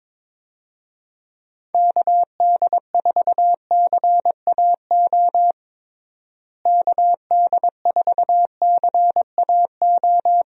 Morse code, heard as a keyed tone, about 22 wpm, 700 Hz. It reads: KD4CAO KD4CAO
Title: USA Call Signs - 6 Characters - Repeated 2x - 22wpm